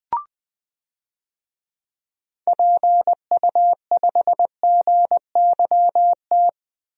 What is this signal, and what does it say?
Morse code, heard as a keyed tone, about 21 wpm, 700 Hz. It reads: PU5GYT
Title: DXCC-100 Call Signs - 6 Characters - Repeated 2x - 20wpm